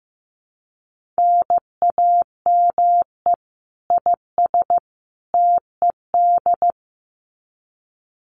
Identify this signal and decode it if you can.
Morse code, heard as a keyed tone, about 15 wpm, 700 Hz. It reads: NAME IS TED
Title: Common QSO Phrases 15wpm